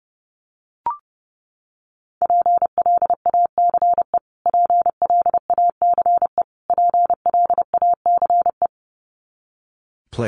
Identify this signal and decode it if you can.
Morse code, heard as a keyed tone, about 30 wpm, 700 Hz. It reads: PLACE PLACE PLACE
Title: Top 200 Words Repeated 30wpm